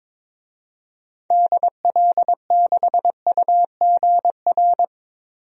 Morse code, heard as a keyed tone, about 21 wpm, 700 Hz. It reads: DL6UGR
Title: DXCC-100 Call Signs - 6 Characters - Repeated 3x - 22wpm